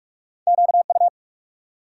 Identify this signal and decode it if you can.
Morse code, heard as a keyed tone, about 60 wpm, spaced as for 53 wpm, 700 Hz. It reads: QU
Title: Speed-Racing Multi-letter Phonemes - 45wpm